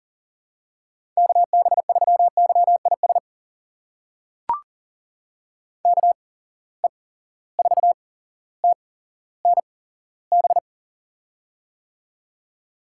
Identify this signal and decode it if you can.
Morse code, heard as a keyed tone, about 40 wpm, spaced as for 17 wpm, 700 Hz. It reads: KB3YIS KE4TNB
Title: USA Call Signs - 6 Characters - 8x Character Spacing - 40wpm